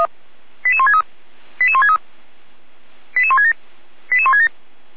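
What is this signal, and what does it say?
Five-tone selective call (ZVEI), heard as quick runs of five tones, tones of about 80 ms. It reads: 88163 88163 88167 88167
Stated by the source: ZVEI five-tone selective call received with an SDR.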